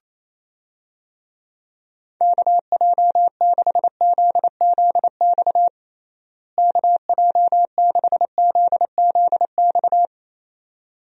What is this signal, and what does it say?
Morse code, heard as a keyed tone, about 28 wpm, 700 Hz. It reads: KJ6ZZX KJ6ZZX
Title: USA Call Signs - 6 Characters - Repeated 2x - 28wpm